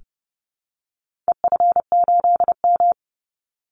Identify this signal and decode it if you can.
Morse code, heard as a keyed tone, about 28 wpm, 700 Hz. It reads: EF8M
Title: Call Signs Repeated 3x - 30wpm